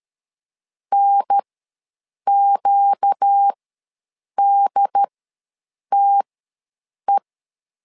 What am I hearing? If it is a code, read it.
Morse code, heard as a keyed tone, about 13 wpm, spaced as for 7 wpm, 800 Hz. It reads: NQDTE
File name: LEZIONI / 4 Q1S3 ws 12wpm 2038